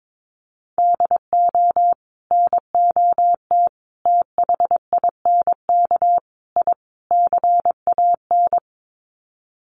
Morse code, heard as a keyed tone, about 22 wpm, 700 Hz. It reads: DO NOT THINK I CAN
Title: Sets of 5 Words Encore2 22wpm